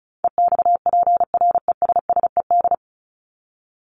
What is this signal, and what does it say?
Morse code, heard as a keyed tone, about 34 wpm, 700 Hz. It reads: EXPRESSED
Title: ED Suffix - Top 100 Words with Base Words 35wpm